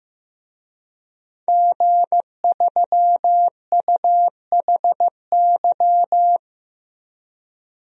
Morse code, heard as a keyed tone, about 15 wpm, 700 Hz. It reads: G3UHY
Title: DXCC-100 Call Signs - 5 Characters - 15wpm